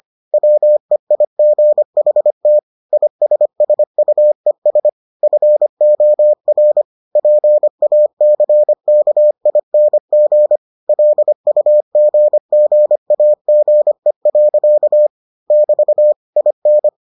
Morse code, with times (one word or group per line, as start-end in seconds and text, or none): 0.3-2.6 WEIGHT
2.9-4.9 ISSUES
5.2-6.8 FOR
7.2-10.6 PACKING
10.9-15.1 LUGGAGE.
15.5-16.1 <BT>
16.4-16.9 IN